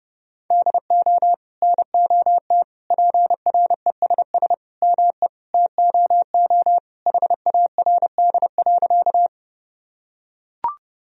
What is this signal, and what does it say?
Morse code, heard as a keyed tone, about 30 wpm, 700 Hz. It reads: DO NOT PRESS ME TOO HARD.
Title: Sentences from Top 1000 Words - Review Part 1 - 30wpm